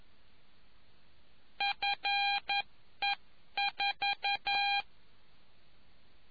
Morse code, heard as a keyed tone, about 11 wpm, 800 Hz. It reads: FE4